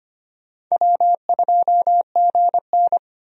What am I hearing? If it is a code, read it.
Morse code, heard as a keyed tone, about 25 wpm, 700 Hz. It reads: W2GN
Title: USA Call Signs - 4 Characters - Repeated 2x - 25wpm